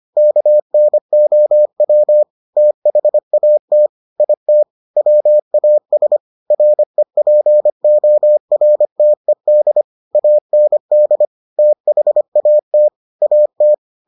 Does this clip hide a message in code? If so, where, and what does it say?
Yes: Morse code from 0.2 to 13.7 s, heard as a keyed tone, about 25 wpm, 600 Hz. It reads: KNOW THAT IT WAS REPORTED AND THAT AT